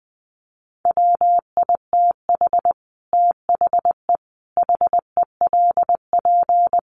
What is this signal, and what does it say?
Morse code, heard as a keyed tone, about 20 wpm, 700 Hz. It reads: WITH THE HELP
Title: Sets of 5 Words 20wpm